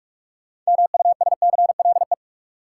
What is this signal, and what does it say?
Morse code, heard as a keyed tone, about 46 wpm, 700 Hz. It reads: MUSCLE
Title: Sentences from Top 1300 Words - New Word Drill - 45wpm